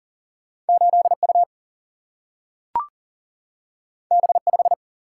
Morse code, heard as a keyed tone, about 40 wpm, 700 Hz. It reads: 8U B5
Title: Single-Letter Single-Number 40wpm